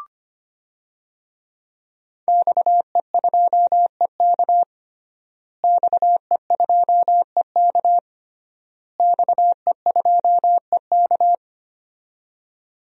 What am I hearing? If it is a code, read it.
Morse code, heard as a keyed tone, about 25 wpm, 700 Hz. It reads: XE2EK XE2EK XE2EK
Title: DXCC-100 Call Signs - 5 Characters - Repeated 3x - 25wpm